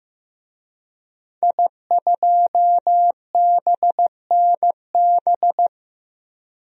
Morse code, heard as a keyed tone, about 15 wpm, 700 Hz. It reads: I2BNB
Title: DXCC-100 Call Signs - 5 Characters - Repeated 2x - 15wpm